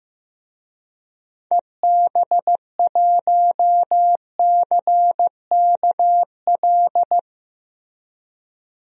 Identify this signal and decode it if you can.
Morse code, heard as a keyed tone, about 16 wpm, 700 Hz. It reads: EB1CKL
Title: DXCC-100 Call Signs - 6 Characters - Repeated 2x - 15wpm